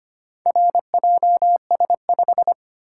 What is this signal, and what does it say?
Morse code, heard as a keyed tone, about 25 wpm, 700 Hz. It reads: RJS5
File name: practice / 25wpm / Four Letter-Number Combinations - 25wpm